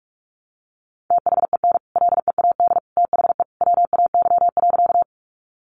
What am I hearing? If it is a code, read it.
Morse code, heard as a keyed tone, about 44 wpm, 700 Hz. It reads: THEN LEAD THE WAY.
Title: Sentences from Top 500 Words 45wpm